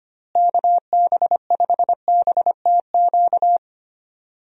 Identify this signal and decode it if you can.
Morse code, heard as a keyed tone, about 25 wpm, 700 Hz. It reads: KB5BTQ
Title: USA Call Signs - 6 Characters - Repeated 3x - 25wpm